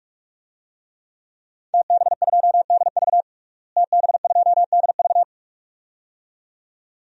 Morse code, heard as a keyed tone, about 46 wpm, 700 Hz. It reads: TB2DV TB2DV
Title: DXCC-100 Call Signs - 5 Characters - Repeated 2x - 45wpm